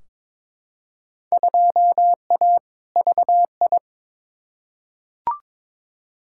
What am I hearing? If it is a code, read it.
Morse code, heard as a keyed tone, about 22 wpm, 700 Hz. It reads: 2A VI
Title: Field Day Exchanges - 22wpm